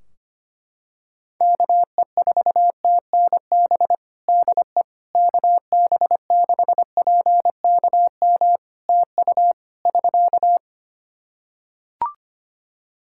Morse code, heard as a keyed tone, about 25 wpm, 700 Hz. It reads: KE4TNB DE KB6PKM TU <SK>
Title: Practice QSO - Exchange 4 - Ending - 25wpm